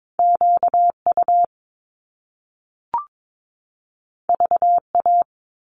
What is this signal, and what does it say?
Morse code, heard as a keyed tone, about 22 wpm, 700 Hz. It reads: QU VA
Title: Top Two Letter Combinations in Words 22wpm